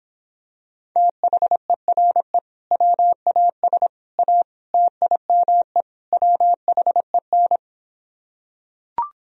Morse code, heard as a keyed tone, about 26 wpm, 700 Hz. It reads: THERE WAS A TIME WHEN